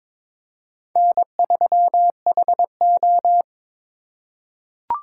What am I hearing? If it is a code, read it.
Morse code, heard as a keyed tone, about 22 wpm, 700 Hz. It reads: N3HO
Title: USA Call Signs - 4 Characters - 4x Character Spacing - 22wpm